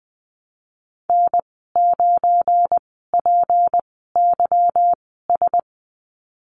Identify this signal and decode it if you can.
Morse code, heard as a keyed tone, about 20 wpm, spaced as for 15 wpm, 700 Hz. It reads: N9PYS
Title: 186 Lesson 30g - Letter b - Call Signs 20wpm